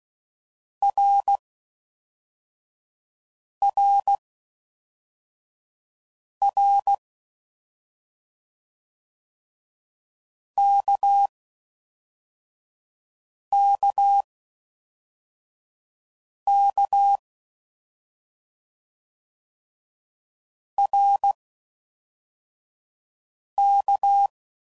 Morse code, heard as a keyed tone, about 16 wpm, spaced as for 4 wpm, 800 Hz. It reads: RRR KKK RK